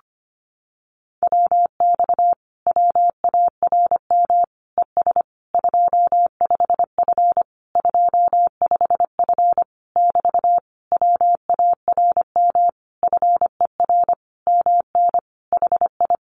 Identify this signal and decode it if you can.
Morse code, heard as a keyed tone, about 25 wpm, 700 Hz. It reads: WX WARM ES 25F 25F <BT> WARM FER MN HI
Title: Common QSO Phrases Encore 25wpm